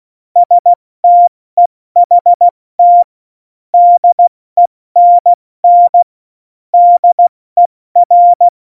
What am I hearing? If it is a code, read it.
Morse code, heard as a keyed tone, about 16 wpm, 700 Hz. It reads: STEHT DENN DER
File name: Morsekurs / Witze / Witz 26. September 17, Tempo 70